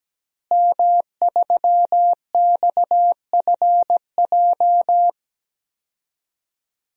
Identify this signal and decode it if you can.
Morse code, heard as a keyed tone, about 17 wpm, 700 Hz. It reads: M3XFJ
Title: DXCC-100 Call Signs - 5 Characters - Repeated 2x - 17wpm